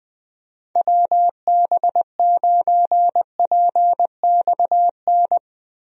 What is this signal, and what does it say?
Morse code, heard as a keyed tone, about 21 wpm, 700 Hz. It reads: WB9PXN
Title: USA Call Signs - 6 Characters - Repeated 3x - 20wpm